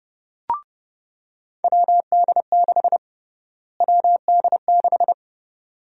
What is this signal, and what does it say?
Morse code, heard as a keyed tone, about 32 wpm, 700 Hz. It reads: WD6 WD6
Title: USA Call Sign Prefixes - Repeated 2x - 30wpm